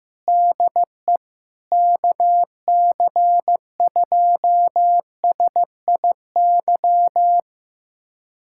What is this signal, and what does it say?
Morse code, heard as a keyed tone, about 15 wpm, 700 Hz. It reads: DE KC2SIY
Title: USA Callsign DE Callsign 15wpm